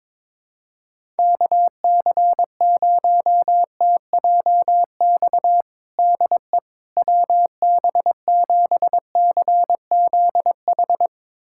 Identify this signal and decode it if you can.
Morse code, heard as a keyed tone, about 22 wpm, 700 Hz. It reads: KC0TJX DE WB7CZH
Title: USA Callsign DE Callsign 22wpm